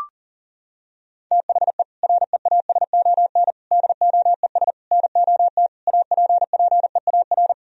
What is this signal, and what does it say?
Morse code, heard as a keyed tone, about 40 wpm, 700 Hz. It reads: THE REASON DOES NOT APPEAR
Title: Sentences from Top 1000 Words - Review Part 3 - 40wpm